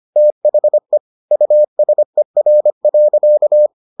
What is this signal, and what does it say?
Morse code, heard as a keyed tone, about 25 wpm, 600 Hz. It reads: THE USER.